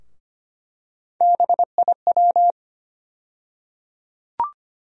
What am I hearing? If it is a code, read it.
Morse code, heard as a keyed tone, about 25 wpm, 700 Hz. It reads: BIW
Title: USA Call Sign Postfixes - 3 Letters 25wpm